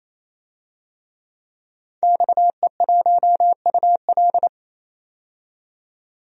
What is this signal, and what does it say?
Morse code, heard as a keyed tone, about 28 wpm, 700 Hz. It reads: XE1UL
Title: DXCC-100 Call Signs - 5 Characters - 28wpm